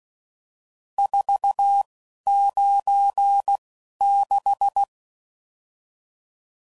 Morse code, heard as a keyed tone, about 16 wpm, spaced as for 11 wpm, 800 Hz. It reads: 496
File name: CW Code Groups @10WPM / RSTs / Various RST No's